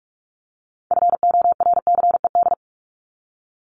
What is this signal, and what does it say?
Morse code, heard as a keyed tone, about 44 wpm, 700 Hz. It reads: FORCED
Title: ED Suffix - Top 100 Words with Base Words 45wpm